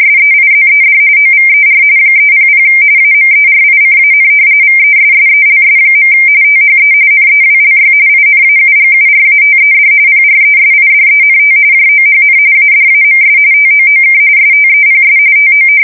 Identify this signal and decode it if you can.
RTTY, about 45.45 baud. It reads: SEIZED OVER 300 AMERICAN VESSELS. THE NEWLY INAUGURATED PRESIDENT JOHN ADAMS TOOK STEPS TO
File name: fsk170 / usnavy1